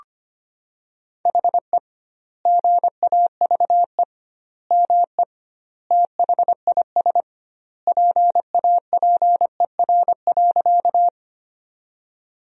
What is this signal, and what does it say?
Morse code, heard as a keyed tone, about 25 wpm, 700 Hz. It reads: HE GAVE ME THIS PAPER.
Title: 2x Word-Spacing - Sentences from Top 500 Words - 25wpm